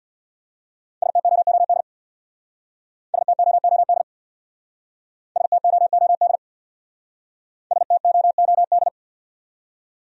Morse code, heard as a keyed tone, about 60 wpm, 700 Hz. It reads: STOOD STOOD STOOD STOOD
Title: Speed-Racing Top 500 Words 50wpm